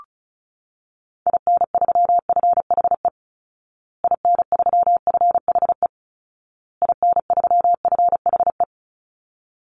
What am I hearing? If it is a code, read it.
Morse code, heard as a keyed tone, about 34 wpm, 700 Hz. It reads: IN3FHE IN3FHE IN3FHE
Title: Call Signs Repeated 2x - 35wpm